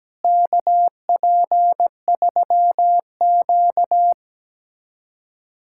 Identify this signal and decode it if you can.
Morse code, heard as a keyed tone, about 17 wpm, 700 Hz. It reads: KP3Q
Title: USA Callsign DE Callsign - 4 Characters - 17wpm